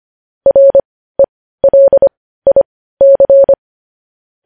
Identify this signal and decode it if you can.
Morse code, heard as a keyed tone, about 25 wpm, spaced as for 15 wpm, 550 Hz. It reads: RELIC
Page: News Headlines in Morse Code at 15 WPM